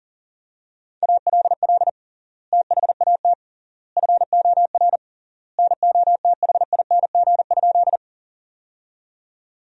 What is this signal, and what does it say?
Morse code, heard as a keyed tone, about 40 wpm, 700 Hz. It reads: ALL THAT FOR NOTHING?
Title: 3x Word-Spacing - Sentences from Top 500 Words - 40wpm